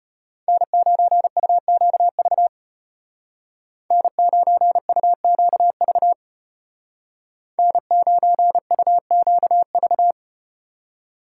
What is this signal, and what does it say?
Morse code, heard as a keyed tone, about 34 wpm, 700 Hz. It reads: N9UQV N9UQV N9UQV